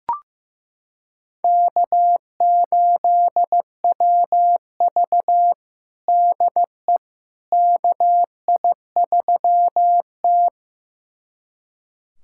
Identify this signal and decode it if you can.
Morse code, heard as a keyed tone, about 15 wpm, 700 Hz. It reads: K8WV DE KI3T
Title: USA Callsign DE Callsign - 4 Characters - 15wpm